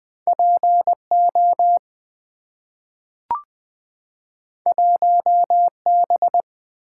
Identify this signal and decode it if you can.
Morse code, heard as a keyed tone, about 20 wpm, 700 Hz. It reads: P O 1 B
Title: Two Letter-Number Combinations - 20wpm